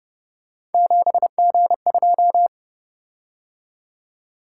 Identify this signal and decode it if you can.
Morse code, heard as a keyed tone, about 30 wpm, 700 Hz. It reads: 7G2
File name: Speed-Racing - Three Letter-Number Combinations - 25wpm